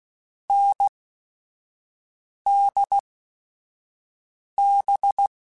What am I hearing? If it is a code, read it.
Morse code, heard as a keyed tone, about 16 wpm, spaced as for 5 wpm, 800 Hz. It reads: NDB
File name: Short Runs / Short Runs-mp3s / NDB NDB6 @5